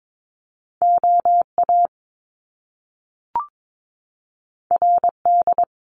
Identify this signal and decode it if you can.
Morse code, heard as a keyed tone, about 22 wpm, 700 Hz. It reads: OA RD
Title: Top Two Letter Combinations in Words 22wpm